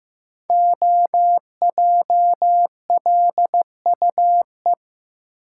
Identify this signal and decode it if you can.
Morse code, heard as a keyed tone, about 15 wpm, 700 Hz. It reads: OJLUE
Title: Code Groups - 5 Letter:Numbers - 15wpm